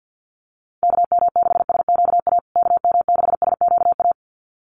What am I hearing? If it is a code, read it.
Morse code, heard as a keyed tone, about 50 wpm, 700 Hz. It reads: KM6SQA KM6SQA
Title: USA Call Signs Repeated 2x 50wpm